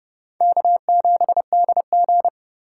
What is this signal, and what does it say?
Morse code, heard as a keyed tone, about 32 wpm, 700 Hz. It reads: K7DG
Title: USA Call Signs - 4 Characters - 3x Character Spacing - 30wpm